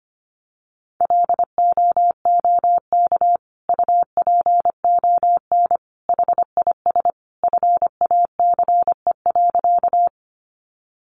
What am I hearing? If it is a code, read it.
Morse code, heard as a keyed tone, about 25 wpm, 700 Hz. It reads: LOOK UPON HIS FACE.
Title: Sentences from Top 1100 Words 25wpm